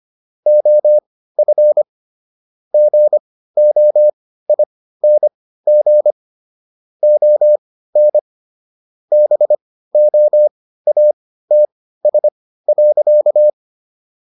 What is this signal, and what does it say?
Morse code, heard as a keyed tone, about 25 wpm, spaced as for 15 wpm, 600 Hz. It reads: OF GOING ON BOATS.